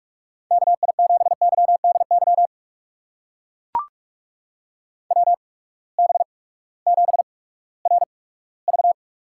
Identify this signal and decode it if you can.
Morse code, heard as a keyed tone, about 44 wpm, spaced as for 20 wpm, 700 Hz. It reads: KI7YDY WB7RV
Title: USA Call Signs - 6 Characters - 8x Character Spacing - 45wpm